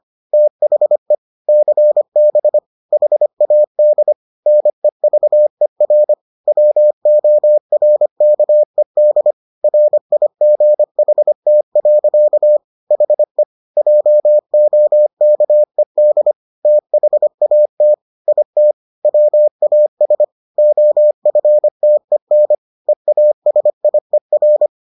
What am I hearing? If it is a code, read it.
Morse code, heard as a keyed tone, about 25 wpm, 600 Hz. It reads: THE CB HAD NEVER WORKED RIGHT. HE JOKED THAT IT WAS OFTEN EASIER